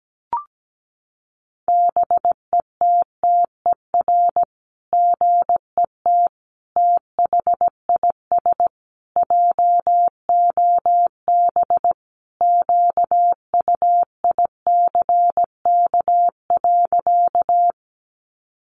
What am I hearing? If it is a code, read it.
Morse code, heard as a keyed tone, about 17 wpm, 700 Hz. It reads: BETTER GET THIS JOB QUICK.
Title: Sentences from Top 600 Words 17wpm